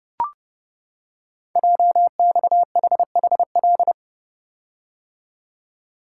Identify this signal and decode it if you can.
Morse code, heard as a keyed tone, about 30 wpm, 700 Hz. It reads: JXHHL
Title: Code Groups - 5 Letters - 30wpm